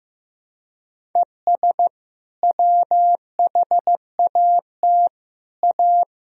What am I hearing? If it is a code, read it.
Morse code, heard as a keyed tone, about 15 wpm, 700 Hz. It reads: ES WHAT A
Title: Sentences from Top 500 Words and Abbreviations 15wpm